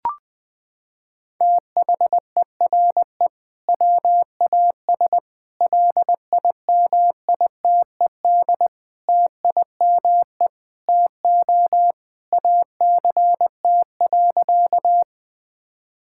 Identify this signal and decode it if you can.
Morse code, heard as a keyed tone, about 20 wpm, 700 Hz. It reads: THERE WAS LIMITED TIME TO ACT.
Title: Sentences from Top 1000 Words - Review Part 6 - 20wpm